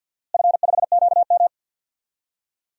Morse code, heard as a keyed tone, about 48 wpm, 700 Hz. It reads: F5QM
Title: DXCC-100 Call Signs - 4 Characters - Repeated 2x - 50wpm